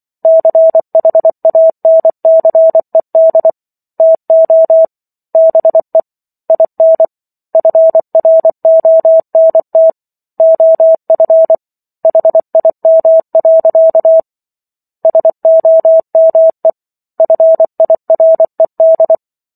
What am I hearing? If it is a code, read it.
Morse code, heard as a keyed tone, about 24 wpm, 650 Hz. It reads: CHANCED TO BE IN FRONT OF HIM. SOME FIRED